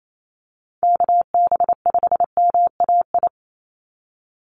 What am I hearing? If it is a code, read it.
Morse code, heard as a keyed tone, about 28 wpm, 700 Hz. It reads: KB5MAI
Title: USA Call Signs 28wpm